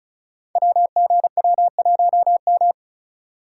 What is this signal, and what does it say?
Morse code, heard as a keyed tone, about 36 wpm, 700 Hz. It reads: WGW1M
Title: Code Groups - 5 Letter:Numbers - 35wpm